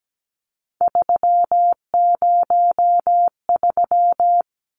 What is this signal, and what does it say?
Morse code, heard as a keyed tone, about 17 wpm, 700 Hz. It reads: 303
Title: Numbers - 3-Digit 17wpm